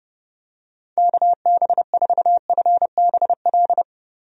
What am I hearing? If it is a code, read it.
Morse code, heard as a keyed tone, about 32 wpm, 700 Hz. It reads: KB4FBL
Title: USA Call Signs - 6 Characters - 30wpm